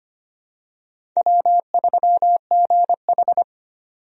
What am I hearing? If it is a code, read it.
Morse code, heard as a keyed tone, about 25 wpm, 700 Hz. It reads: W3GH
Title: USA Call Signs - 4 Characters - 25wpm